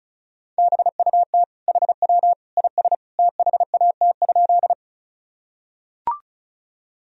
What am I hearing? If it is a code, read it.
Morse code, heard as a keyed tone, about 34 wpm, 700 Hz. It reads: BUT HW IS THAT?
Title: Sentences from Top 100 Words and Abbreviations 35wpm